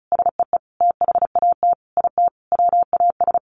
Morse code, heard as a keyed tone, about 34 wpm, 700 Hz. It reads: SEE THAT IT WAS
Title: Sentences from Top 800 Words 35wpm